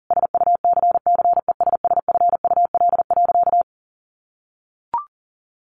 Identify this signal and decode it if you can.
Morse code, heard as a keyed tone, about 40 wpm, 700 Hz. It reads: SUCCESSFUL.
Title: Sentences from Top 1300 Words 40wpm